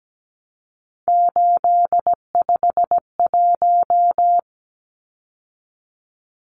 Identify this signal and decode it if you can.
Morse code, heard as a keyed tone, about 17 wpm, 700 Hz. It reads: 851
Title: Numbers - 3-Digit 17wpm